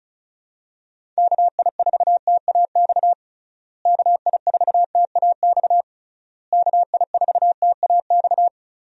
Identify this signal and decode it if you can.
Morse code, heard as a keyed tone, about 34 wpm, 700 Hz. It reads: KI4TAX KI4TAX KI4TAX
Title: USA Call Signs - 6 Characters - Repeated 3x - 35wpm